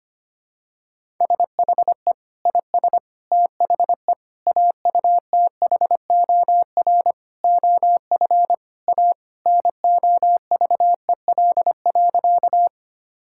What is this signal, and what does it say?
Morse code, heard as a keyed tone, about 25 wpm, 700 Hz. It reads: SHE IS THE AUTHOR OF A NOVEL.
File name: practice / 25wpm / Sentences with 4-Word Phrases - 25wpm